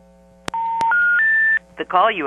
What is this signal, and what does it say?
Special information tone, with speech.
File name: TurtleCreekEWSD